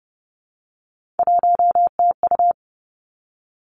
Morse code, heard as a keyed tone, about 30 wpm, 700 Hz. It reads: 1TU
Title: Three Challenging Letter-Number Combinations 30wpm